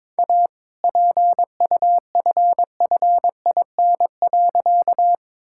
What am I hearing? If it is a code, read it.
Morse code, heard as a keyed tone, about 22 wpm, 690 Hz. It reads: A PUFFIN.
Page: morse-code-ninja-practice-22wpm